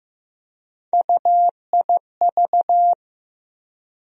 Morse code, heard as a keyed tone, about 15 wpm, 700 Hz. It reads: UIV
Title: USA Call Sign Postfixes - 3 Letters 15wpm